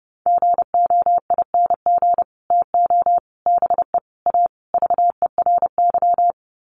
Morse code, heard as a keyed tone, about 30 wpm, 700 Hz. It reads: GOING TO BE A VERY
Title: Sets of 5 Words Encore 30wpm